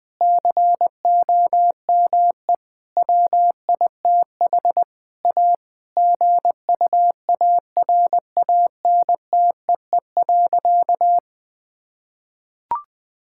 Morse code, heard as a keyed tone, about 21 wpm, 700 Hz. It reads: COME WITH A GUARANTEE.
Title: Sentences from Top 1600 Words 20wpm